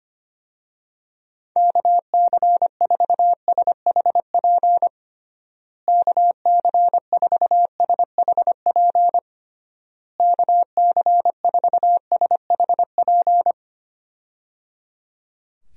Morse code, heard as a keyed tone, about 25 wpm, 700 Hz. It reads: KC4SHP KC4SHP KC4SHP
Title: USA Call Signs - 6 Characters - Repeated 3x - 25wpm